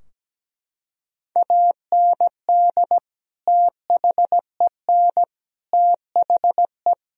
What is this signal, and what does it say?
Morse code, heard as a keyed tone, about 17 wpm, 700 Hz. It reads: AND THEN THE
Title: Sentences from Top 1000 Words - Review Part 3 - 17wpm